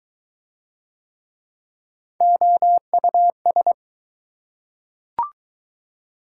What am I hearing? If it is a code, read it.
Morse code, heard as a keyed tone, about 20 wpm, 700 Hz. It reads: OUS